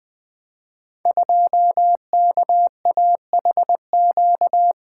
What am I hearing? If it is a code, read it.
Morse code, heard as a keyed tone, about 20 wpm, 700 Hz. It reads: 2KAHQ
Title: Code Groups - 5 Letter:Numbers - 20wpm